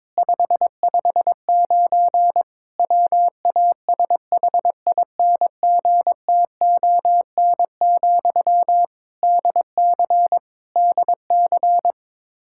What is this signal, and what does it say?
Morse code, heard as a keyed tone, about 22 wpm, 700 Hz. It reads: 559 WASHINGTON, DC DC
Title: Practice QSO - Exchange 1 - Beginning - 22wpm